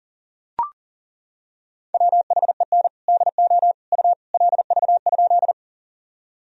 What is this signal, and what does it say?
Morse code, heard as a keyed tone, about 40 wpm, 700 Hz. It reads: WHEN DO U LV?
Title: Sentences from Top 400 Words and Abbreviations 40wpm